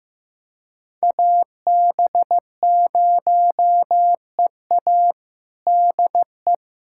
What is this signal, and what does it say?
Morse code, heard as a keyed tone, about 16 wpm, 700 Hz. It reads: AB0EA DE
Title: USA Callsign DE Callsign - 5 Characters - 15wpm